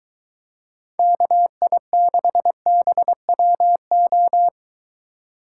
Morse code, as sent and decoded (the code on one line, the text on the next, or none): -.- .. -.... -... .-- ---
KI6BWO